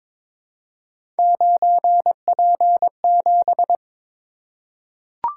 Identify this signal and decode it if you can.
Morse code, heard as a keyed tone, about 22 wpm, 700 Hz. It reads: 9P7
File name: practice / 22wpm / Three Letter-Number Combinations - 22wpm